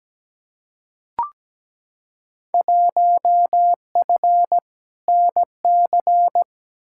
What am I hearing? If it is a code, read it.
Morse code, heard as a keyed tone, about 17 wpm, 700 Hz. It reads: 1F NC
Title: Field Day Exchanges - 17wpm